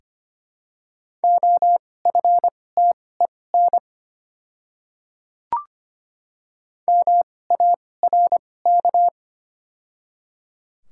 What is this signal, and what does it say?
Morse code, heard as a keyed tone, about 25 wpm, spaced as for 18 wpm, 700 Hz. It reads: OFTEN MARK
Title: 2x Character Spacing - Top 300 Words - 25wpm